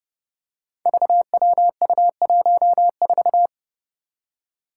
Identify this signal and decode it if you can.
Morse code, heard as a keyed tone, about 30 wpm, 700 Hz. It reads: VWU14
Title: Five Challenging Letter-Number Combinations 30wpm